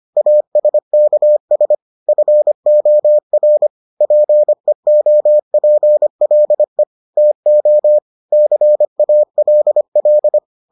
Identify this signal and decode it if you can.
Morse code, heard as a keyed tone, about 25 wpm, 600 Hz. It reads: ASKS FOR PEOPLE TO CALL